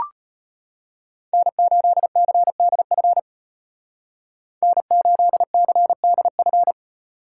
Morse code, heard as a keyed tone, about 33 wpm, 700 Hz. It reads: N8CDF N8CDF